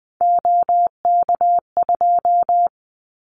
Morse code, heard as a keyed tone, about 20 wpm, 700 Hz. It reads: OK2
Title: Callsign Prefixes for Top 25 DXCC - Repeated 3x - 20wpm